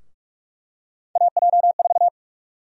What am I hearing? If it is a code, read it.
Morse code, heard as a keyed tone, about 44 wpm, 700 Hz. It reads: AJ4
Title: USA Call Sign Prefixes - Repeated 2x - 45wpm